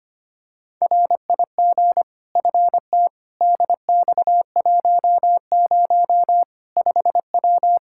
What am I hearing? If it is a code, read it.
Morse code, heard as a keyed tone, about 25 wpm, 700 Hz. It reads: RIG FT DX10 5W
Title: Practice QSO - Exchange 2 - Info - 25wpm